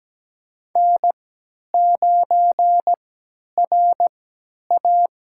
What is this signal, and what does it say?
Morse code, heard as a keyed tone, about 17 wpm, spaced as for 10 wpm, 700 Hz. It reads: N9RA
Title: USA Call Signs - 4 Characters - 3x Character Spacing - 17wpm